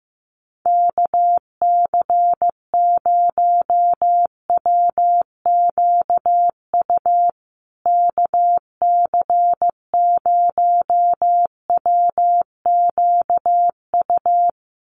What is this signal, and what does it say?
Morse code, heard as a keyed tone, about 15 wpm, 700 Hz. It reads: KC0WQU KC0WQU
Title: USA Call Signs Repeated 2x 15wpm